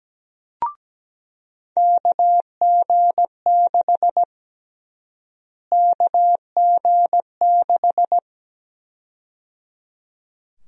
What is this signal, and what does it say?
Morse code, heard as a keyed tone, about 17 wpm, 700 Hz. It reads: KG6 KG6
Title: USA Call Sign Prefixes - Repeated 2x - 17wpm